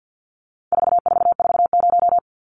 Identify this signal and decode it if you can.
Morse code, heard as a keyed tone, about 50 wpm, 700 Hz. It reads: 4440
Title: Numbers - 4-Digit 50wpm